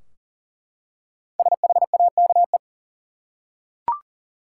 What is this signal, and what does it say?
Morse code, heard as a keyed tone, about 40 wpm, 700 Hz. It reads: SHAKE
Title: Five Letter Words - 40wpm